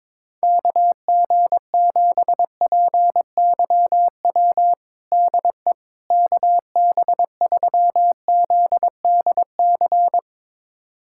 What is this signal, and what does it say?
Morse code, heard as a keyed tone, about 22 wpm, 700 Hz. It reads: KG7PYW DE KB3ZDC
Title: USA Callsign DE Callsign - 6 Characters - 22wpm